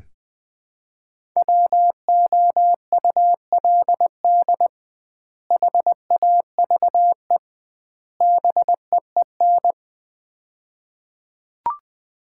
Morse code, heard as a keyed tone, about 20 wpm, 700 Hz. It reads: WOULD HAVE BEEN
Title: 2x Word-Spacing - Sets of 3 Words - 20wpm